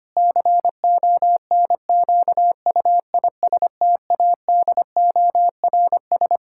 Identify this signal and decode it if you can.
Morse code, heard as a keyed tone, about 25 wpm, 700 Hz. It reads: CONQUISTADORS
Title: Pangram Challenge v2 - 25wpm